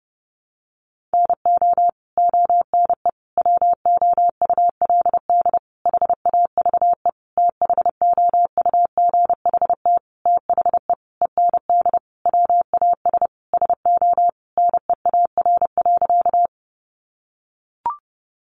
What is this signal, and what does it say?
Morse code, heard as a keyed tone, about 30 wpm, 700 Hz. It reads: NO ONE WOULD HAVE THOUGHT THE END WAS SO NEAR.
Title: Sentences from Top 300 Words 30wpm